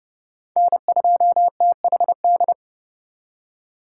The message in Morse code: N2THD

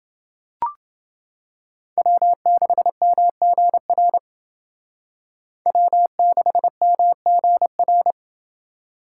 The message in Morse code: W6MGR W6MGR

